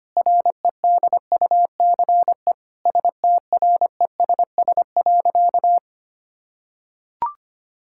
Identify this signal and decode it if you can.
Morse code, heard as a keyed tone, about 25 wpm, 700 Hz. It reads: REDUCE STRESS.
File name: practice / 25wpm / Difficult Sentences with Hints - 25wpm